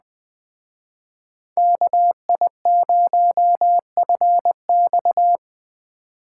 Morse code, heard as a keyed tone, about 20 wpm, 700 Hz. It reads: KI0FX